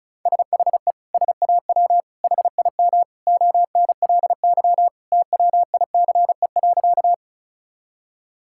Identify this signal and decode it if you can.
Morse code, heard as a keyed tone, about 34 wpm, 700 Hz. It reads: SHE SAW HIM ONLY TWICE.
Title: Sentences from Top 1600 Words 35wpm